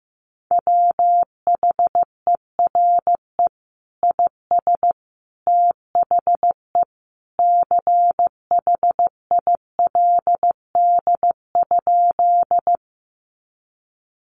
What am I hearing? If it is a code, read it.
Morse code, heard as a keyed tone, about 15 wpm, 700 Hz. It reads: WHERE IS THE CHILD?
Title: Sentences from Top 800 Words 15wpm